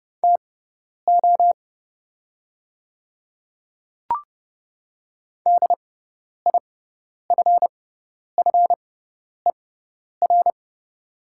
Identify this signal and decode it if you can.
Morse code, heard as a keyed tone, about 30 wpm, spaced as for 10 wpm, 700 Hz. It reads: TO DIFFER
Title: 6x Character Spacing - Top 200 Words - 30wpm